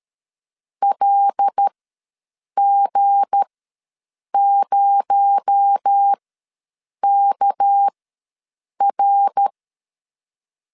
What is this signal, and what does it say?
Morse code, heard as a keyed tone, about 13 wpm, spaced as for 7 wpm, 800 Hz. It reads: LG0KR
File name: LEZIONI / 9 2867 ws 12wpm 3431